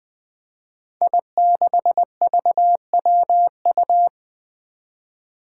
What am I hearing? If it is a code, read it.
Morse code, heard as a keyed tone, about 20 wpm, 700 Hz. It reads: I6VWU
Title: Five Challenging Letter-Number Combinations 20wpm